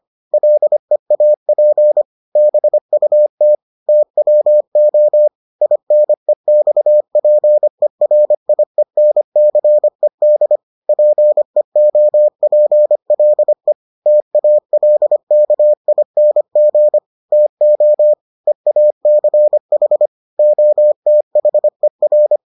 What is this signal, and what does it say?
Morse code, heard as a keyed tone, about 25 wpm, 600 Hz. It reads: LEAP BUT TWO INEXPERIENCED PEOPLE TALKING TO EACH OTHER